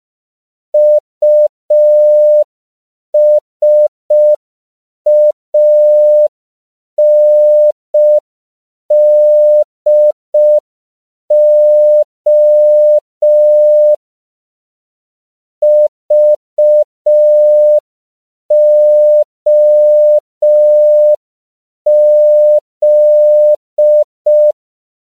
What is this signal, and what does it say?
Morse code, heard as a keyed tone, about 5 wpm, 600 Hz. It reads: USANDO VOZ
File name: CW / 5PPM